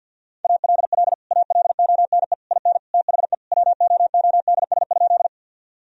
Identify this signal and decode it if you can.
Morse code, heard as a keyed tone, about 50 wpm, 700 Hz. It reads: ALL ALONE IN THE WOODS?
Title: Sentences from Top 1000 Words - Review Part 2 - 50wpm